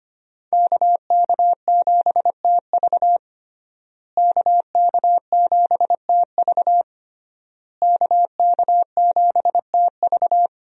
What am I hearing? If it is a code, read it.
Morse code, heard as a keyed tone, about 25 wpm, 700 Hz. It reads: KK7TV KK7TV KK7TV
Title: USA Call Signs - 5 Characters - Repeated 3x - 25wpm